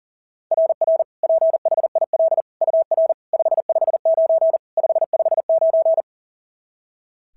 Morse code, heard as a keyed tone, about 40 wpm, 690 Hz. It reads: RR PHIL UR 559 559
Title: SOTA QSO Practice 40wpm